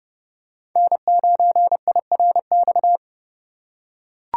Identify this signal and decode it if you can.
Morse code, heard as a keyed tone, about 29 wpm, 700 Hz. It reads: N9IRX